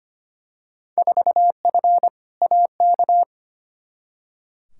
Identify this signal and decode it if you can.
Morse code, heard as a keyed tone, about 25 wpm, 700 Hz. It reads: 4F AK
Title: Field Day Exchanges Encore - 25wpm